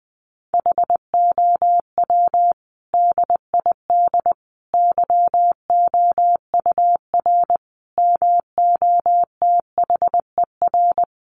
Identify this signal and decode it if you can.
Morse code, heard as a keyed tone, about 20 wpm, 700 Hz. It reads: HOW DID YOUR MOTHER
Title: Sentences from Top 700 Words 20wpm